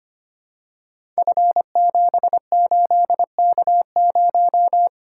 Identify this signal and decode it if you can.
Morse code, heard as a keyed tone, about 25 wpm, 700 Hz. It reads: F78K0
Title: Code Groups - 5 Letter:Numbers - 25wpm